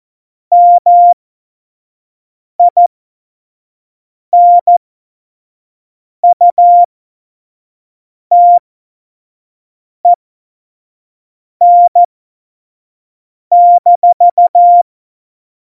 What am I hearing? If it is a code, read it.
Morse code, heard as a keyed tone, about 14 wpm, spaced as for 5 wpm, 700 Hz. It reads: MINUTEN-
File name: Morsekurs / Witze / Witz 19. September 17, Tempo 25